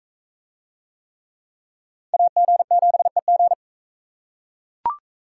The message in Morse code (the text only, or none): AG7EG